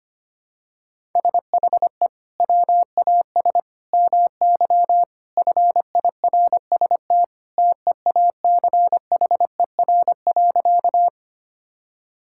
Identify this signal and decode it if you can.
Morse code, heard as a keyed tone, about 25 wpm, 700 Hz. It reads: SHE WAS MY FIRST TEACHER.
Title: Sentences from Top 1000 Words - Review Part 6 - 25wpm